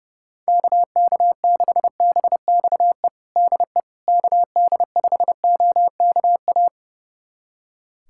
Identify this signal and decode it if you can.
Morse code, heard as a keyed tone, about 32 wpm, 700 Hz. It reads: KK6BXE DE KD5OKA
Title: USA Callsign DE Callsign - 6 Characters - 30wpm